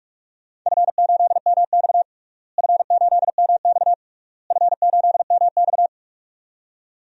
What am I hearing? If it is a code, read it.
Morse code, heard as a keyed tone, about 46 wpm, 700 Hz. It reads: F8MX F8MX F8MX
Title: DXCC-100 Call Signs - 4 Characters - Repeated 3x - 45wpm